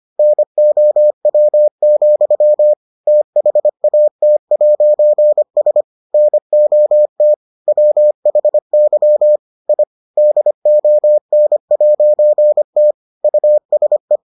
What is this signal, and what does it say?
Morse code, heard as a keyed tone, about 25 wpm, 600 Hz. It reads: NOW, THAT'S NOT WHY I DON'T USE